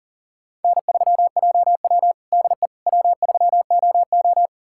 Morse code, heard as a keyed tone, about 40 wpm, 700 Hz. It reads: N3JW DE W3OO
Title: USA Callsign DE Callsign - 4 Characters - 40wpm